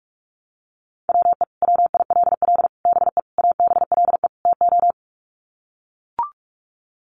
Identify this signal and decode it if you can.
Morse code, heard as a keyed tone, about 44 wpm, 700 Hz. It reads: WE WILL BE ABLE TO
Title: Sets of 5 Words Encore 45wpm